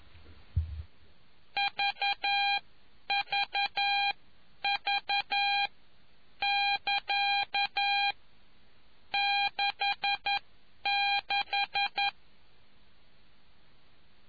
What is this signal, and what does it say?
Morse code, heard as a keyed tone, about 11 wpm, spaced as for 9 wpm, 2400 Hz. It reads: VVV <KA> 66